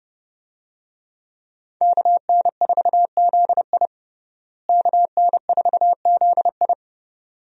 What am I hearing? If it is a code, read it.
Morse code, heard as a keyed tone, about 32 wpm, 700 Hz. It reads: KN4ZI KN4ZI
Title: USA Call Signs - 5 Characters - Repeated 2x - 30wpm